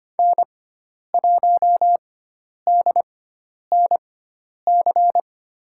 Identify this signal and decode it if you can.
Morse code, heard as a keyed tone, about 25 wpm, spaced as for 10 wpm, 700 Hz. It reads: N1DNC
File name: Lesson 16g - Letter c - Call Signs 25wpm